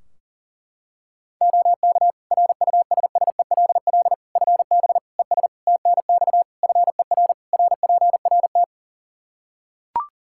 Morse code, heard as a keyed tone, about 37 wpm, 700 Hz. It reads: OK RUSSELL FB ES TNX FER RPRT
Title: Practice QSO - Exchange 2 - Info - 40wpm